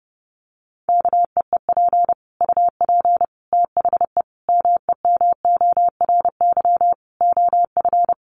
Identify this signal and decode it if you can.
Morse code, heard as a keyed tone, about 30 wpm, 700 Hz. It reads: KEEP UP THE MEMORY OF
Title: Sentences from Top 1300 Words 30wpm